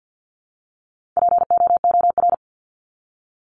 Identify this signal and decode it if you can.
Morse code, heard as a keyed tone, about 50 wpm, 700 Hz. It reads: POOR
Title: Top 751-1000 Words 50wpm